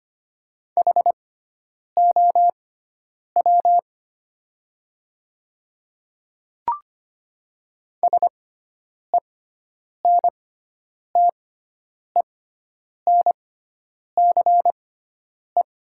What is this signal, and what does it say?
Morse code, heard as a keyed tone, about 25 wpm, spaced as for 9 wpm, 700 Hz. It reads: HOW SENTENCE
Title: 6x Character Spacing - Top 500 Words - 25wpm